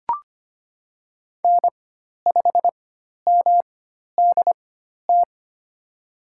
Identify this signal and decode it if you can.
Morse code, heard as a keyed tone, about 25 wpm, spaced as for 12 wpm, 700 Hz. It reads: N5MDT
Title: USA Call Signs - 5 Characters - 4x Character Spacing - 25wpm